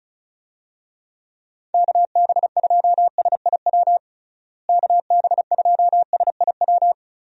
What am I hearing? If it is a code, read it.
Morse code, heard as a keyed tone, about 36 wpm, 700 Hz. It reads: KB2SIW KB2SIW
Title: USA Call Signs - 6 Characters - Repeated 3x - 35wpm